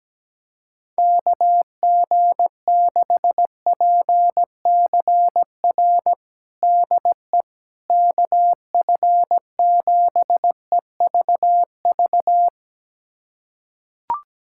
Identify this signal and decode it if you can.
Morse code, heard as a keyed tone, about 17 wpm, 700 Hz. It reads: KG6PCR DE KF7EVV
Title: USA Callsign DE Callsign - 6 Characters - 17wpm